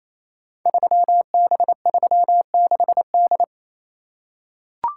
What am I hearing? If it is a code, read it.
Morse code, heard as a keyed tone, about 27 wpm, 700 Hz. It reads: 3B36D
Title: Code Groups - 5 Letter:Numbers - 28wpm